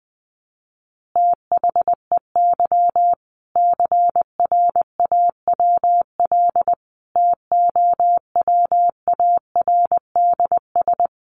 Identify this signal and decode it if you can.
Morse code, heard as a keyed tone, about 20 wpm, 700 Hz. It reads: THEY CRAWL TOWARDS
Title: Easy Phrases 20wpm